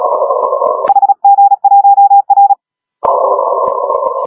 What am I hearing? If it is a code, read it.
Morse code, heard as a keyed tone, about 34 wpm, 810 Hz, with noise. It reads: HG1R
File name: HG1R CW SSB MIX